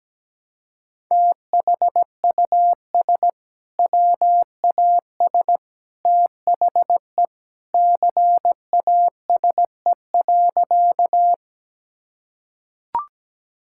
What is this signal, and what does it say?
Morse code, heard as a keyed tone, about 17 wpm, 700 Hz. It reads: THUS WAS THE CASE.
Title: Sentences from Top 1000 Words - Review Part 5 - 17wpm